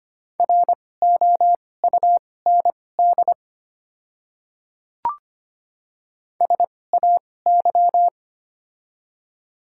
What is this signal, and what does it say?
Morse code, heard as a keyed tone, about 25 wpm, spaced as for 18 wpm, 700 Hz. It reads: ROUND SAY
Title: 2x Character Spacing - Top 200 Words - 25wpm